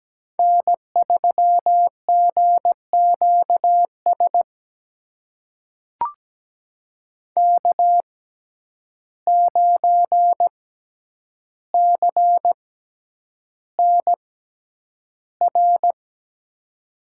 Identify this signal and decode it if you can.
Morse code, heard as a keyed tone, about 17 wpm, spaced as for 9 wpm, 700 Hz. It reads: N3GQS K9CNR
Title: USA Call Signs - 5 Characters - 6x Character Spacing - 17wpm